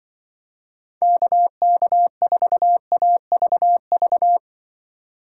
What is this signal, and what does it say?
Morse code, heard as a keyed tone, about 25 wpm, 700 Hz. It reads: KK4AVV
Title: Speed Racing - USA 2x3 Call Signs 20wpm